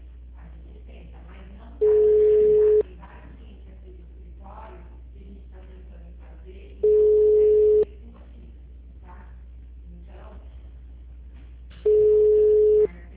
Ringback tone (Europe, 425 Hz).
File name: out- -2036- - - .67